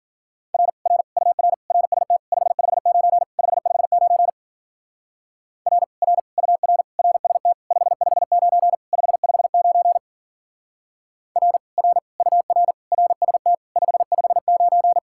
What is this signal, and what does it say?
Morse code, heard as a keyed tone, about 47 wpm, 700 Hz. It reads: R R UR RST 559 559 R R UR RST 559 559 R R UR RST 559
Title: Speed-Racing Common QSO Phrases - 40wpm